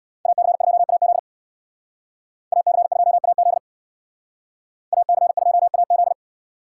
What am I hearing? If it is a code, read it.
Morse code, heard as a keyed tone, about 61 wpm, 700 Hz. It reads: AC2AZ AC2AZ AC2AZ
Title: Speed Racing - USA 2x2 Call Signs 50wpm